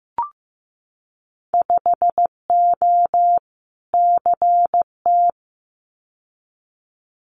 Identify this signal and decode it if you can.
Morse code, heard as a keyed tone, about 15 wpm, 700 Hz. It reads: 5O CT
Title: Winter Field Day Exchanges Encore 15wpm